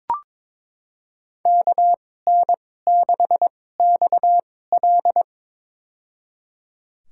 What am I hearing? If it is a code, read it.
Morse code, heard as a keyed tone, about 21 wpm, spaced as for 16 wpm, 700 Hz. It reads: KN6XL
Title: USA Call Signs - 5 Characters - 2x Character Spacing - 22wpm